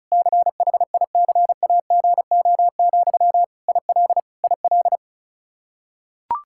Morse code, heard as a keyed tone, about 34 wpm, 700 Hz. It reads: CHICAGO, IL IL
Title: Practice QSO - Exchange 1 - Beginning - 35wpm